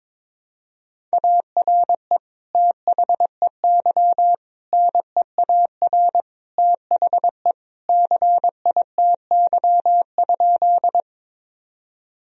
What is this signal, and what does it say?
Morse code, heard as a keyed tone, about 22 wpm, 700 Hz. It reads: ARE THEY NEAR THE CITY?
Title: Sentences from Top 1000 Words - Review Part 1 - 22wpm